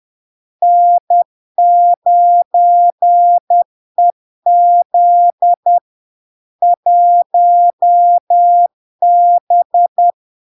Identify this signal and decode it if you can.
Morse code, heard as a keyed tone, about 10 wpm, 700 Hz. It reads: N9EZ 1B
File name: LARK-FD-2022-Log-in-CW-10wpm-700-Hz